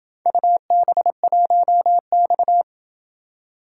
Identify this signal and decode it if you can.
Morse code, heard as a keyed tone, about 24 wpm, 700 Hz. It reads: UB1X
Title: Speed-Racing - Four Letter-Number Combinations - 20wpm